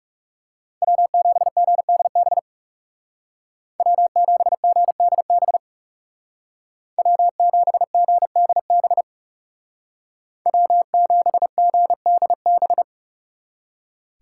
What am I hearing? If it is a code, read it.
Morse code, heard as a keyed tone, about 37 wpm, 700 Hz. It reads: W7GDB W7GDB W7GDB W7GDB
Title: Speed Racing - USA Call Signs 30wpm